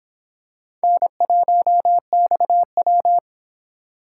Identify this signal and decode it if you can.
Morse code, heard as a keyed tone, about 26 wpm, 700 Hz. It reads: N1XW